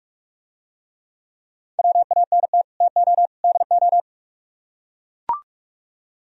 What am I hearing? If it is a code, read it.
Morse code, heard as a keyed tone, about 44 wpm, 700 Hz. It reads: WANT TO DO